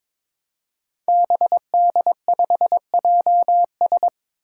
Morse code, heard as a keyed tone, about 22 wpm, 700 Hz. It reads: BD5JS
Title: Five Challenging Letter-Number Combinations 22wpm